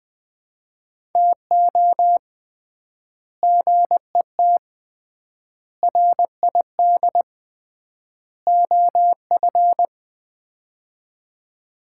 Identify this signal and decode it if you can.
Morse code, heard as a keyed tone, about 21 wpm, 700 Hz. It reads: TO GET RID OF
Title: 3x Word-Spacing - Sets of 4 Words - 20wpm